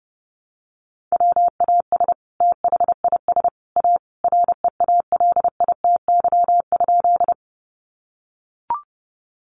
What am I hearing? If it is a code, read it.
Morse code, heard as a keyed tone, about 30 wpm, 700 Hz. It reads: WAS THIS A REALITY?
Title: Sentences from Top 1200 Words 30wpm